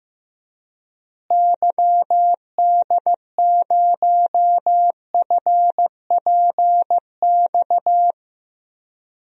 Morse code, heard as a keyed tone, about 15 wpm, 700 Hz. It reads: YD0FPX
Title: DXCC-100 Call Signs - 6 Characters - Repeated 3x - 15wpm